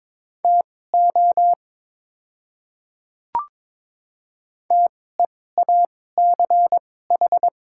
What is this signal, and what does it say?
Morse code, heard as a keyed tone, about 22 wpm, spaced as for 16 wpm, 700 Hz. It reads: TO TEACH
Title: 2x Character Spacing - Top 500 Words - 22wpm